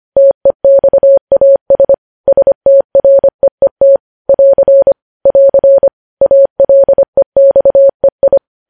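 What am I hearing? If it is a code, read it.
Morse code, heard as a keyed tone, about 25 wpm, 550 Hz. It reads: TEXAS STREET <AR> <AR> ALEXEI
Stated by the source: Morse code news headlines sent at 25 wpm.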